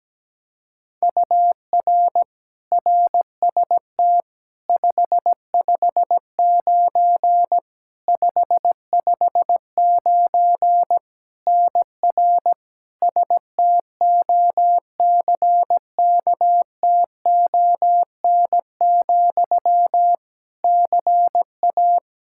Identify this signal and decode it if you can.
Morse code, heard as a keyed tone, about 17 wpm, 700 Hz. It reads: UR RST 559 559 NR STOCKTON, CA
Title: Practice QSO - Exchange 1 - Beginning - 17wpm